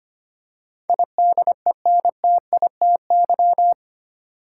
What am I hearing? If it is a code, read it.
Morse code, heard as a keyed tone, about 25 wpm, 700 Hz. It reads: IDENTITY
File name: Eight Letter Words - 25wpm